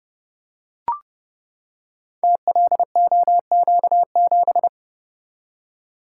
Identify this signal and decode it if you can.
Morse code, heard as a keyed tone, about 30 wpm, 700 Hz. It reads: TLOQ7
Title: Code Groups - 5 Letter:Numbers - 30wpm